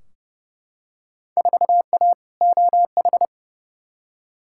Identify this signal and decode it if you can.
Morse code, heard as a keyed tone, about 32 wpm, 700 Hz. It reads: 4A OH
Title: Field Day Exchanges - 30wpm